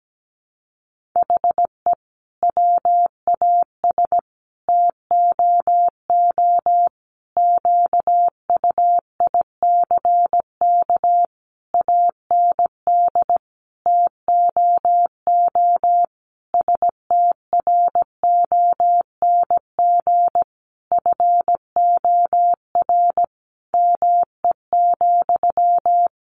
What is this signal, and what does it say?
Morse code, heard as a keyed tone, about 17 wpm, 700 Hz. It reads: HE WAS TOO QUICK AND TOO STRONG FOR ME,
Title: Sentences from Top 600 Words 17wpm